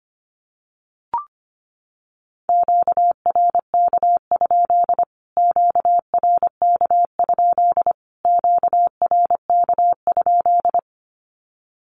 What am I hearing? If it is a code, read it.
Morse code, heard as a keyed tone, about 25 wpm, 700 Hz. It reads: QRK? QRK? QRK?
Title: Q-Codes Repeated 3x 25wpm